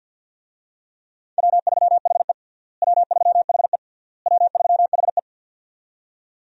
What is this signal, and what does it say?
Morse code, heard as a keyed tone, about 53 wpm, 700 Hz. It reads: W3HE W3HE W3HE
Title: USA Call Signs - 4 Characters - Repeated 3x - 50wpm